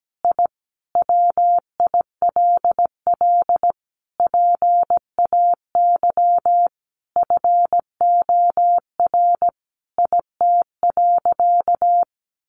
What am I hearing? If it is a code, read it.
Morse code, heard as a keyed tone, about 17 wpm, 700 Hz. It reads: I WILL PAY FOR IT.
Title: Sentences from Top 1300 Words 17wpm